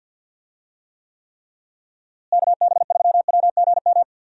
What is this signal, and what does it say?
Morse code, heard as a keyed tone, about 50 wpm, 700 Hz. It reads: KB3WGM
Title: USA Call Signs - 6 Characters - Repeated 3x - 50wpm